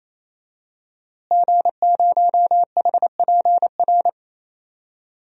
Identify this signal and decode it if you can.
Morse code, heard as a keyed tone, about 27 wpm, 700 Hz. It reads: G0HPR
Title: DXCC-100 Call Signs - 5 Characters - Repeated 3x - 28wpm